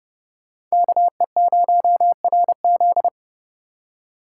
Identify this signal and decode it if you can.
Morse code, heard as a keyed tone, about 38 wpm, 700 Hz. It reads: KE0RZ